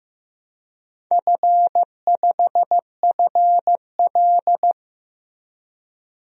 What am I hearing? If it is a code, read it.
Morse code, heard as a keyed tone, about 15 wpm, 700 Hz. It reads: F5FL
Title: DXCC-100 Call Signs - 4 Characters - Repeated 2x - 15wpm